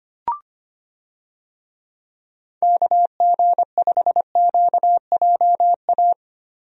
Morse code, heard as a keyed tone, about 25 wpm, 700 Hz. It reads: KG5QJA
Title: USA Call Signs - 6 Characters - Repeated 3x - 25wpm